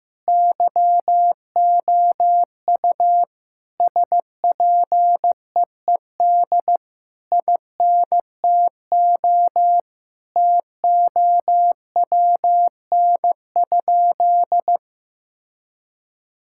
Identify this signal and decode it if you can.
Morse code, heard as a keyed tone, about 15 wpm, 700 Hz. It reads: YOU SPEED INTO TOWN?
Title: Sentences from Top 1000 Words - Review Part 5 - 15wpm